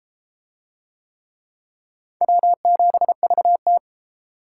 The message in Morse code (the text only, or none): W7VT